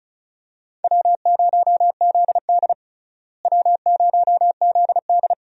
Morse code, heard as a keyed tone, about 36 wpm, 700 Hz. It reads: W0ZD W0ZD
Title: USA Call Signs - 4 Characters - Repeated 2x - 35wpm